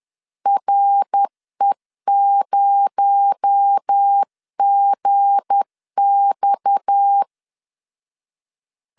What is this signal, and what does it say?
Morse code, heard as a keyed tone, about 10 wpm, 800 Hz. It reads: RE0GX
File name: LEZIONI / 7 GF9Z / 7 G 10wpm 2501